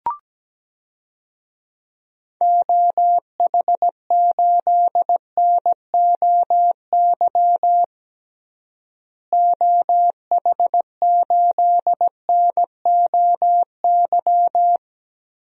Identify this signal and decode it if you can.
Morse code, heard as a keyed tone, about 17 wpm, 700 Hz. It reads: OH8NOY OH8NOY
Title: DXCC-100 Call Signs - 6 Characters - Repeated 3x - 17wpm